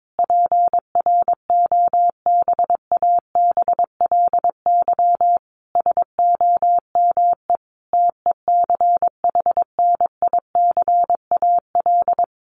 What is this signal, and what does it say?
Morse code, heard as a keyed tone, about 22 wpm, 700 Hz. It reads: PROBABLY SOME TECHNICAL
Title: Sentences from Top 1100 Words 22wpm